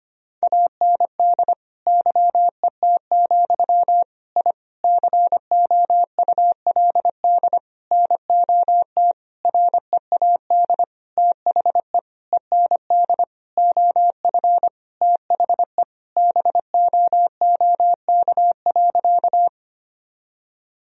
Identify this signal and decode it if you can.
Morse code, heard as a keyed tone, about 25 wpm, 700 Hz. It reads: AND YET, I COULD NOT READ THE END OF THE BOOK.
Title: Sentences from Top 1000 Words - Review Part 1 - 25wpm